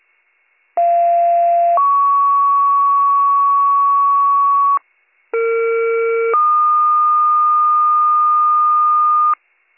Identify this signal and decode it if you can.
Two-tone page, heard as a one-second tone, then a three-second tone, about 690 then 1120 Hz, about 460 then 1220 Hz. It is heard from 0.8 to 9.3 s.